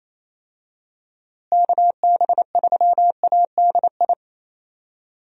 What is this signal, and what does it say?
Morse code, heard as a keyed tone, about 28 wpm, 700 Hz. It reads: KB3ADI
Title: USA Call Signs - 6 Characters - Repeated 2x - 28wpm